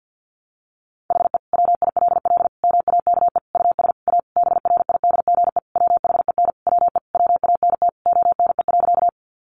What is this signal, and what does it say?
Morse code, heard as a keyed tone, about 50 wpm, 700 Hz. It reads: HE WILL MAKE US A BRIDGE WHEN WE WANT ONE.
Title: Sentences from Top 1200 Words 50wpm